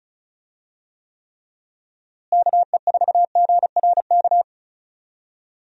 Morse code, heard as a keyed tone, about 36 wpm, 700 Hz. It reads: KE4GRK
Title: USA Call Signs - 6 Characters - Repeated 2x - 35wpm